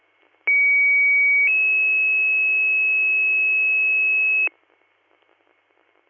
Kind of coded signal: two-tone page

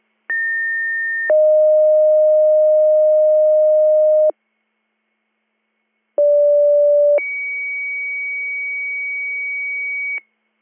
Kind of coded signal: two-tone page

